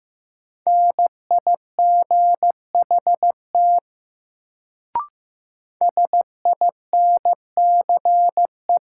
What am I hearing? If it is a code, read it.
Morse code, heard as a keyed tone, about 15 wpm, 700 Hz. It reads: NIGHT SINCE
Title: Sentences from Top 300 Words - New Word Drill - 15wpm